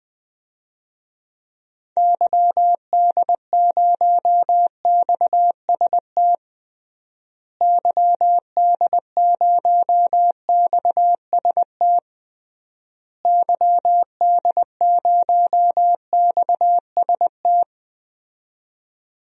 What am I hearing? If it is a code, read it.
Morse code, heard as a keyed tone, about 20 wpm, 700 Hz. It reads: YD0XST YD0XST YD0XST
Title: DXCC-100 Call Signs - 6 Characters - Repeated 3x - 20wpm